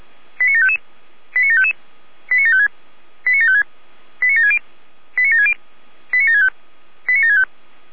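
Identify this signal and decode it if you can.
Five-tone selective call (ZVEI), heard as quick runs of five tones, tones of about 70 ms. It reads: 89755 89755 89756 89756 89760 89760 89765 89765
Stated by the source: ZVEI five-tone selective call 89706 received on SDR.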